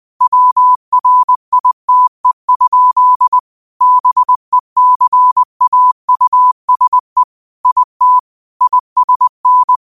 Morse code, heard as a keyed tone, about 20 wpm, 1000 Hz. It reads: WRITE? BECAUSE IT ISN